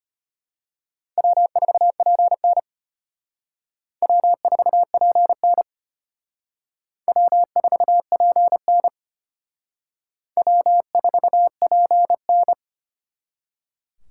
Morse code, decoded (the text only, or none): W4PN W4PN W4PN W4PN